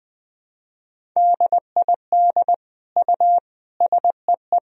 Morse code, heard as a keyed tone, about 20 wpm, 700 Hz. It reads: DID U SEE
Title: Sentences from Top 100 Words and Abbreviations 20wpm